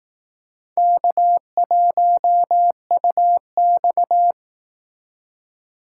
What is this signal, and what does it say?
Morse code, heard as a keyed tone, about 18 wpm, 700 Hz. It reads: K1UX
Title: Speed Racing - USA 1x2 Call Signs 15wpm